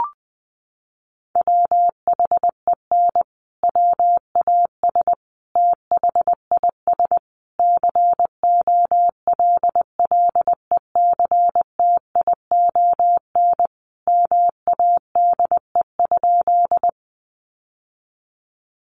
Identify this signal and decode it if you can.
Morse code, heard as a keyed tone, about 20 wpm, 700 Hz. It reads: WHEN WAS THIS COLLECTION MADE?
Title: Sentences from Top 1200 Words 20wpm